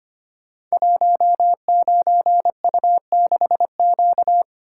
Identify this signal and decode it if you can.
Morse code, heard as a keyed tone, about 25 wpm, 690 Hz. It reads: 19U6Q
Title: Code Groups - 5 Letter:Numbers - 25wpm